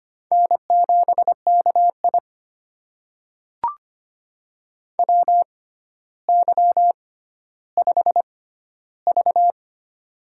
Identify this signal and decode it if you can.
Morse code, heard as a keyed tone, about 25 wpm, spaced as for 13 wpm, 700 Hz. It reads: N7KI WY5V
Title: USA Call Signs - 4 Characters - 6x Character Spacing - 25wpm